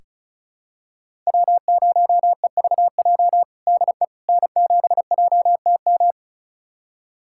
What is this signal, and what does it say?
Morse code, heard as a keyed tone, about 34 wpm, 700 Hz. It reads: W0EVJ DE N7JTM
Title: USA Callsign DE Callsign - 5 Characters - 35wpm